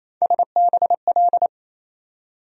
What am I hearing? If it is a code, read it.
Morse code, heard as a keyed tone, about 28 wpm, 700 Hz. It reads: SBL
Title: USA Call Sign Postfixes - 3 Letters 28wpm